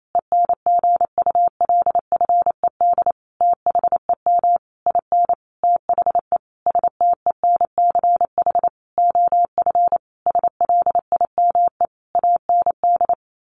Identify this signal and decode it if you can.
Morse code, heard as a keyed tone, about 28 wpm, 700 Hz. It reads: ENGULFED THEM IN THE STENCH OF SLIME AND
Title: A Sound Of Thunder 28wpm